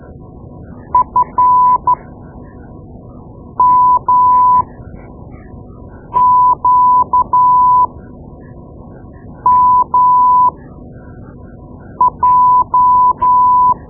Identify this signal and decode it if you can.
Morse code, heard as a keyed tone, about 13 wpm, spaced as for 5 wpm, 1000 Hz. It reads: FMQMJ
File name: Morse Lessons / Morse Lessons / Tape-2B